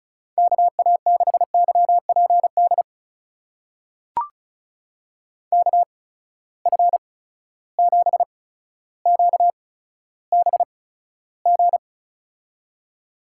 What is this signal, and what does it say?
Morse code, heard as a keyed tone, about 36 wpm, spaced as for 15 wpm, 700 Hz. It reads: KA6YPD KF7QBG
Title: USA Call Signs - 6 Characters - 8x Character Spacing - 35wpm